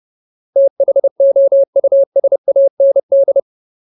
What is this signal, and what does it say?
Morse code, heard as a keyed tone, about 30 wpm, 550 Hz. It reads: THOUSAND
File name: Ultimate Word List / 30WPM Ultimate